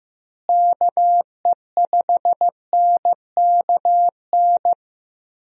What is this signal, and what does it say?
Morse code, heard as a keyed tone, about 15 wpm, 700 Hz. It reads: KE5NKN
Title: USA Call Signs - 6 Characters - Repeated 3x - 15wpm